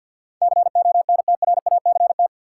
Morse code, heard as a keyed tone, about 53 wpm, 700 Hz. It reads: CONTRACT
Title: Sentences from Top 1400 Words - New Word Drill Alt - 50wpm